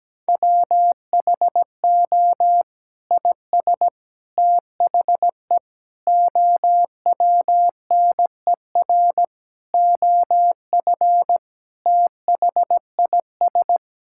Morse code, heard as a keyed tone, about 17 wpm, 700 Hz. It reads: WHO IS THE OWNER OF THIS
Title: Sentences from Top 1600 Words 17wpm